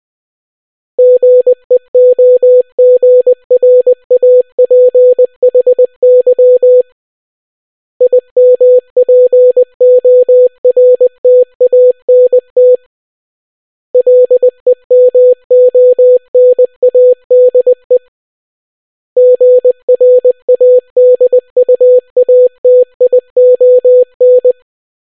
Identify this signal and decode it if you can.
Morse code, heard as a keyed tone, about 20 wpm, 500 Hz. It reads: GEOGRAPHY IMPORTANT LEMONADE GRADUATION